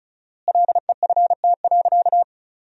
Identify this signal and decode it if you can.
Morse code, heard as a keyed tone, about 36 wpm, 700 Hz. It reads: LEFT.
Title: 4x Word-Spacing - Sentences from Top 300 Words - 35wpm